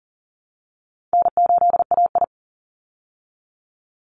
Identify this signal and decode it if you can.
Morse code, heard as a keyed tone, about 39 wpm, 700 Hz. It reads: N8AI
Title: Call Signs Repeated 2x - 40wpm